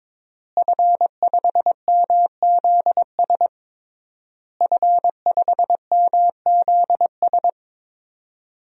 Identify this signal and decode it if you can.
Morse code, heard as a keyed tone, about 22 wpm, 700 Hz. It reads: F5MZS F5MZS
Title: DXCC-100 Call Signs - 5 Characters - Repeated 3x - 22wpm